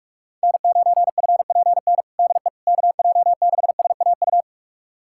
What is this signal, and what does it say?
Morse code, heard as a keyed tone, about 48 wpm, 700 Hz. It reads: N9FPN DE KJ6SAU
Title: USA Callsign DE Callsign 45wpm